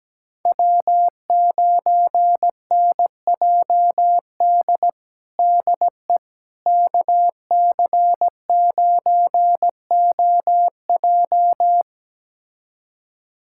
Morse code, heard as a keyed tone, about 17 wpm, 700 Hz. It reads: W9NJD DE KC9OJ
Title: USA Callsign DE Callsign - 5 Characters - 17wpm